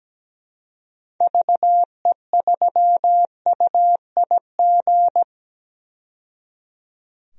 Morse code, heard as a keyed tone, about 17 wpm, 700 Hz. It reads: VE3UIG
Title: DXCC-100 Call Signs - 6 Characters - 17wpm